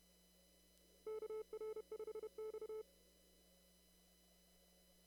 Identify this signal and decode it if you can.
Morse code, heard as a keyed tone, about 31 wpm, 870 Hz. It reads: KR5X
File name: CW / kt5j wpx cw-op-k5pi